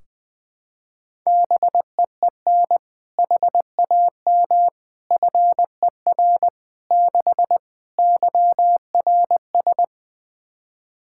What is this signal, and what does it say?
Morse code, heard as a keyed tone, about 20 wpm, 700 Hz. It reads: BEEN HAM FER 6 YRS
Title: Practice QSO - Exchange 3 - More Info - 20wpm